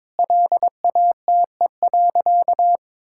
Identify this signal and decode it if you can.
Morse code, heard as a keyed tone, about 22 wpm, 700 Hz. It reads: LATE.
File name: practice / 22wpm / Sentences with 4-Word Phrases - 22wpm